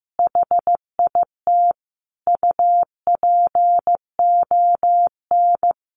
Morse code, heard as a keyed tone, about 15 wpm, 700 Hz. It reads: HIT UPON
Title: Sentences from Top 1200 Words 15wpm